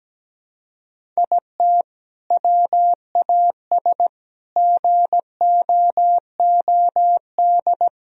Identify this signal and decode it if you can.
Morse code, heard as a keyed tone, about 17 wpm, 700 Hz. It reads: IT WAS GOOD
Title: Sentences from Top 1000 Words - Review Part 1 - 17wpm